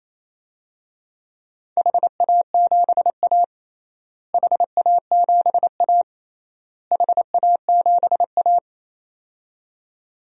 Morse code, heard as a keyed tone, about 28 wpm, 700 Hz. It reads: HA7A HA7A HA7A
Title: DXCC-100 Call Signs - 4 Characters - Repeated 3x - 28wpm